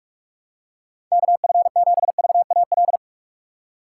Morse code, heard as a keyed tone, about 46 wpm, 700 Hz. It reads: KF7VAL
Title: USA Call Signs - 6 Characters - 2x Character Spacing - 45wpm